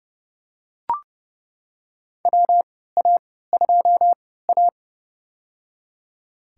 Morse code, heard as a keyed tone, about 31 wpm, spaced as for 17 wpm, 700 Hz. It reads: WA2A
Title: USA Call Signs - 3x Character Spacing - 30wpm